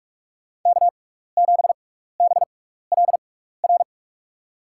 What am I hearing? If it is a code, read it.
Morse code, heard as a keyed tone, about 42 wpm, spaced as for 15 wpm, 700 Hz. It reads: K7BLR
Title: USA Call Signs - 5 Characters - 6x Character Spacing - 45wpm